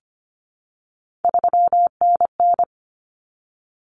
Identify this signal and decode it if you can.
Morse code, heard as a keyed tone, about 25 wpm, 700 Hz. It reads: 3NN
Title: Common Cut Numbers - 3-Digit 25wpm